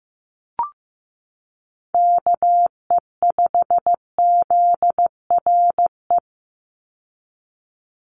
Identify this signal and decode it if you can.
Morse code, heard as a keyed tone, about 15 wpm, 700 Hz. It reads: KE5ZRE
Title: USA Call Signs 15wpm